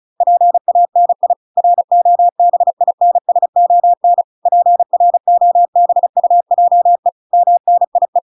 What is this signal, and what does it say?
Morse code, heard as a keyed tone, about 35 wpm, 700 Hz. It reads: PANI ROBINSON PROBUJE MNIE